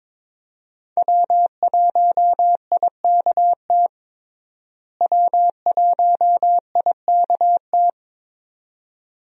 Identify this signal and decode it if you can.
Morse code, heard as a keyed tone, about 22 wpm, 700 Hz. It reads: W1IKT W1IKT
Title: USA Call Signs - 5 Characters - Repeated 2x - 22wpm